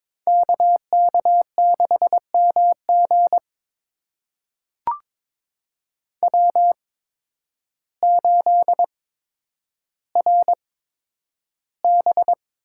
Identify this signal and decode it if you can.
Morse code, heard as a keyed tone, about 22 wpm, spaced as for 10 wpm, 700 Hz. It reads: KK6MG W8RB
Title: USA Call Signs - 5 Characters - 8x Character Spacing - 22wpm